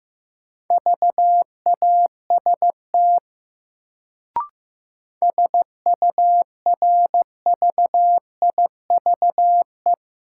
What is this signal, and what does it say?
Morse code, heard as a keyed tone, about 15 wpm, 700 Hz. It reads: VAST SURVIVE
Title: Sentences from Top 1300 Words - New Word Drill - 15wpm